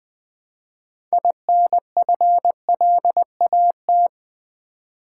Morse code, heard as a keyed tone, about 20 wpm, 700 Hz. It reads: INFLAT
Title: Build A Word - 3 Letters At A Time - 9-Letter Words - 20wpm